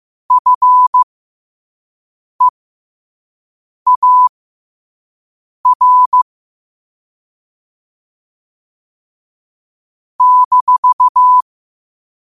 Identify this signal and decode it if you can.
Morse code, heard as a keyed tone, about 15 wpm, spaced as for 5 wpm, 1000 Hz. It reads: FEAR -
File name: qotdfiles / 5wpm